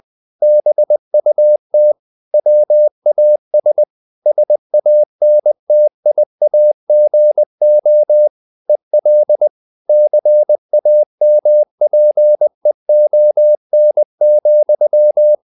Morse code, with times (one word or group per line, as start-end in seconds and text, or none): 0.4-1.9 BUT
2.3-3.8 WAS
4.3-8.3 SANTIAGO
8.7-9.5 EL
9.9-15.3 CAMPEON,